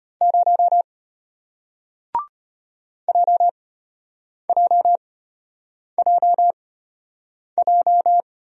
Morse code, heard as a keyed tone, about 33 wpm, spaced as for 8 wpm, 690 Hz. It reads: 0 JJJJ